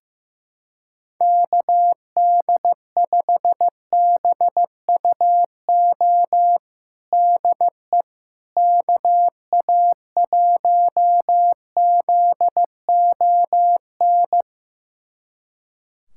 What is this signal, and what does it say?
Morse code, heard as a keyed tone, about 15 wpm, 700 Hz. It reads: KD5BUO DE KA1ZON
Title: USA Callsign DE Callsign - 6 Characters - 15wpm